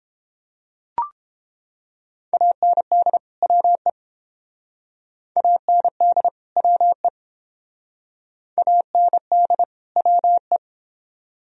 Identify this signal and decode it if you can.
Morse code, heard as a keyed tone, about 30 wpm, 700 Hz. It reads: AND WE AND WE AND WE